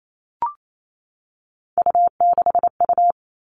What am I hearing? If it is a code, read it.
Morse code, heard as a keyed tone, about 28 wpm, 700 Hz. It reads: U6U
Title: All Cut Numbers - 3-Digit 28wpm